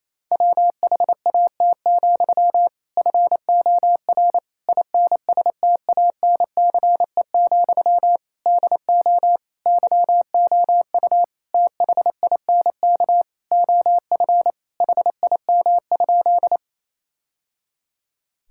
Morse code, heard as a keyed tone, about 28 wpm, 700 Hz. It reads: WHAT, FOR INSTANCE, DO YOU THINK OF HIM?
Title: Sentences from Top 1000 Words - Review Part 5 - 28wpm